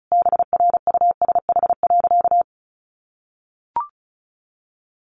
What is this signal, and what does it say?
Morse code, heard as a keyed tone, about 34 wpm, 700 Hz. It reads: BRUSH.
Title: Sentences from Top 1100 Words 35wpm